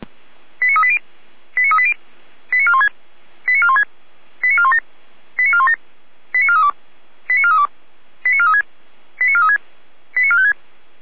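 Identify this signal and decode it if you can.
Five-tone selective call (ZVEI), heard as quick runs of five tones, tones of about 70 ms. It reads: 89380 89380 89416 89416 89417 89417 89432 89432 89436 89436 89456
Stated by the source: ZVEI five-tone selective call received by SDR.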